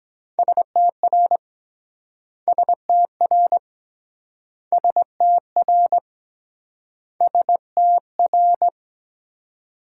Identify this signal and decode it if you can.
Morse code, heard as a keyed tone, about 23 wpm, 700 Hz. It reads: STR STR STR STR